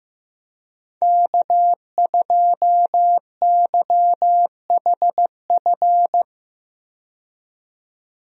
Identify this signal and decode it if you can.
Morse code, heard as a keyed tone, about 15 wpm, 700 Hz. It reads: K2YHF
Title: USA Call Signs - 5 Characters - 15wpm